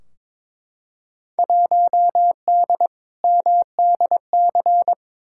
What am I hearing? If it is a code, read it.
Morse code, heard as a keyed tone, about 22 wpm, 700 Hz. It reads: 1D MDC
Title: Field Day Exchanges - 22wpm